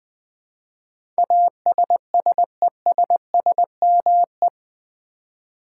Morse code, heard as a keyed tone, about 20 wpm, 700 Hz. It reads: ASSESSME
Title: Build A Word - 4 Letters At A Time - 10-Letter Words - 20wpm